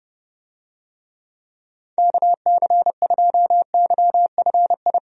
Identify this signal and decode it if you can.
Morse code, heard as a keyed tone, about 30 wpm, 700 Hz. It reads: KC2YFI
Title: USA Call Signs - 6 Characters - 30wpm